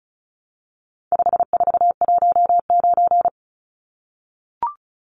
Morse code, heard as a keyed tone, about 34 wpm, 700 Hz. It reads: 5419
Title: Numbers - 4-Digit 35wpm